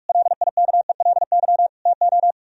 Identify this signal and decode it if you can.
Morse code, heard as a keyed tone, about 54 wpm, 700 Hz. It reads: LIKELY TO
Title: Speed-Racing - Sets of 4 Words 45wpm